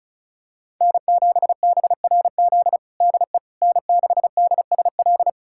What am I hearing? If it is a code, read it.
Morse code, heard as a keyed tone, about 34 wpm, 700 Hz. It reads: N7BRZ DE N6DSL
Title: USA Callsign DE Callsign - 5 Characters - 35wpm